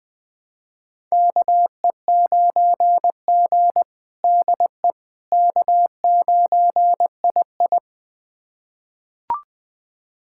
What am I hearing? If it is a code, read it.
Morse code, heard as a keyed tone, about 20 wpm, 700 Hz. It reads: KE9G DE K9II
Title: USA Callsign DE Callsign - 4 Characters - 20wpm